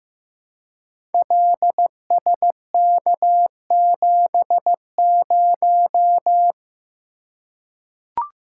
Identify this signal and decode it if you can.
Morse code, heard as a keyed tone, about 15 wpm, 700 Hz. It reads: LSK70
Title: Code Groups - 5 Letter:Numbers - 15wpm